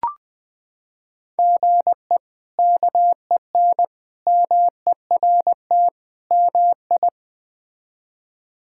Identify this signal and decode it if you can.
Morse code, heard as a keyed tone, about 20 wpm, 700 Hz. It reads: GE KEN MERT MI
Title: K1USN SST QSO Practice - 20wpm